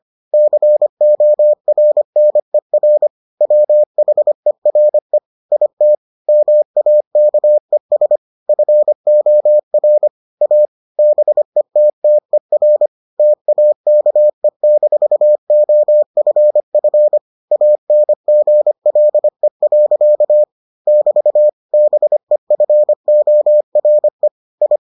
Morse code, heard as a keyed tone, about 25 wpm, 600 Hz. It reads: CORNER WHERE IT MAKES FOR A BETTER TAKE-OFF ANGLE. <BT> BEFORE I